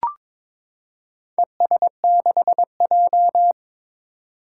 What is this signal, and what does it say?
Morse code, heard as a keyed tone, about 21 wpm, 700 Hz. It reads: ES6J
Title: Four Challenging Letter-Number Combinations 22wpm